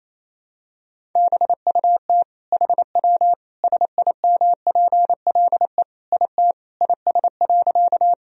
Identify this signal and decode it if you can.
Morse code, heard as a keyed tone, about 28 wpm, 700 Hz. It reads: BUT HW SIMPLE IT IS.
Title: Sentences from Top 400 Words and Abbreviations 28wpm